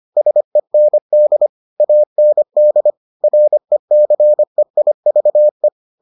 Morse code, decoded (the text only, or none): SEND AND RECEIVE